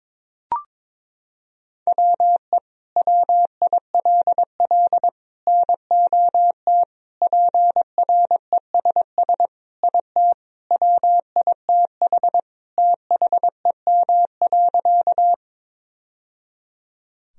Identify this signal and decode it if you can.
Morse code, heard as a keyed tone, about 22 wpm, 700 Hz. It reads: WE WILL NOT PRESS IT WITH THEM.
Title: Sentences from Top 1000 Words - Review Part 1 - 22wpm